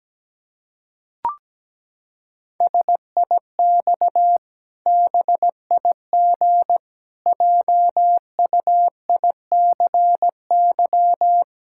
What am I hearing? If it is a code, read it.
Morse code, heard as a keyed tone, about 17 wpm, 700 Hz. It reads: SIX BIG JUICY
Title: Pangram Challenge v2 - 17wpm